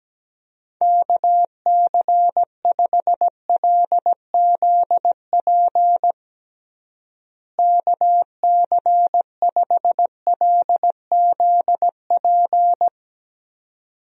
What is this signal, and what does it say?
Morse code, heard as a keyed tone, about 17 wpm, 700 Hz. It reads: KC5LZP KC5LZP
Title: USA Call Signs - 6 Characters - Repeated 2x - 17wpm